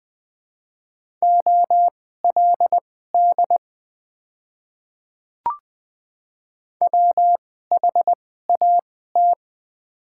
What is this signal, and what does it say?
Morse code, heard as a keyed tone, about 20 wpm, spaced as for 14 wpm, 700 Hz. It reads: OLD WHAT
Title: 2x Character Spacing - Top 200 Words - 20wpm